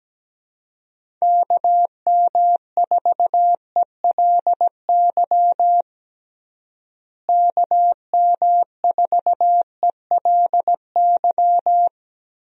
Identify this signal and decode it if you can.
Morse code, heard as a keyed tone, about 17 wpm, 700 Hz. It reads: KM4ELY KM4ELY
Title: USA Call Signs - 6 Characters - Repeated 2x - 17wpm